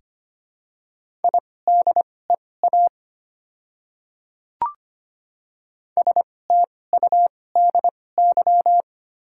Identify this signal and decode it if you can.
Morse code, heard as a keyed tone, about 25 wpm, spaced as for 18 wpm, 700 Hz. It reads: IDEA STUDY
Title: 2x Character Spacing - Top 500 Words - 25wpm